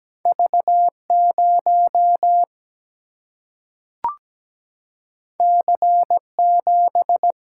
Morse code, heard as a keyed tone, about 17 wpm, 700 Hz. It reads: V0 C7
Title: Single-Letter Single-Number 17wpm